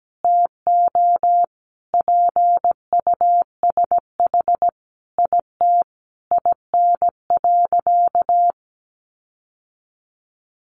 Morse code, heard as a keyed tone, about 17 wpm, 700 Hz. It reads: TO PUSH IT IN.
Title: Sentences from Top 1300 Words 17wpm